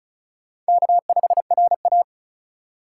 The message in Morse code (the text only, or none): K5RA